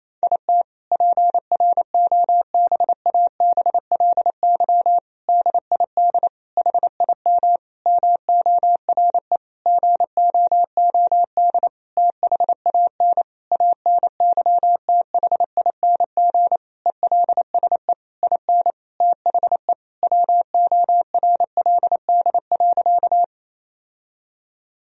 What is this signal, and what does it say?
Morse code, heard as a keyed tone, about 28 wpm, 700 Hz. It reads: IT PROBABLY DID HIM MORE GOOD THAN ANYTHING ELSE IN THE WORLD.
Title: Sentences from Top 1000 Words - Review Part 4 - 28wpm